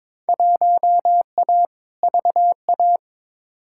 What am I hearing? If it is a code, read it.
Morse code, heard as a keyed tone, about 21 wpm, 700 Hz. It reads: 1A VA
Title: Field Day Exchanges Encore - 22wpm